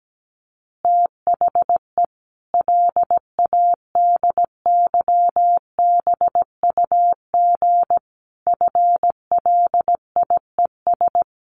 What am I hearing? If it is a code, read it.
Morse code, heard as a keyed tone, about 17 wpm, 700 Hz. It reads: THE LADYBUG FLIES
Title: Additional Easy Phrases 17wpm